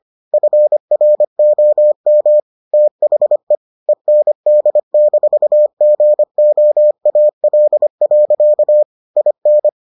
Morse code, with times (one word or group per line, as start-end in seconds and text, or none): 0.3-2.4 FROM
2.7-3.5 THE
3.9-8.8 END-GOAL.
9.2-9.7 IN